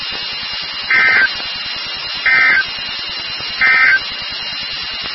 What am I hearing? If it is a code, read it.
Emergency Alert System (SAME) tones, heard as harsh buzzing bursts. It is heard from 0.8 to 4.1 s.
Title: NOAA Weather Radio WXK36